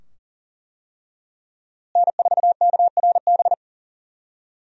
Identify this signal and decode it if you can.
Morse code, heard as a keyed tone, about 37 wpm, 700 Hz. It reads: N4KRB
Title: USA Call Signs - 5 Characters - Repeated 2x - 40wpm